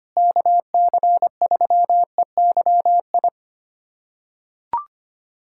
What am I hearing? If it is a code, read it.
Morse code, heard as a keyed tone, about 25 wpm, 700 Hz. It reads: KC3EYI
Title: USA Callsign DE Callsign - 6 Characters - 25wpm